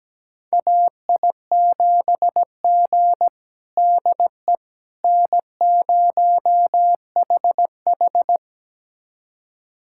Morse code, heard as a keyed tone, about 17 wpm, 700 Hz. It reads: AI7G DE N0HH
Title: USA Callsign DE Callsign - 4 Characters - 17wpm